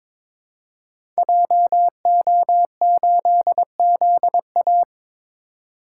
Morse code, heard as a keyed tone, about 22 wpm, 700 Hz. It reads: JO8ZA
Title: Code Groups - 5 Letter:Numbers - 22wpm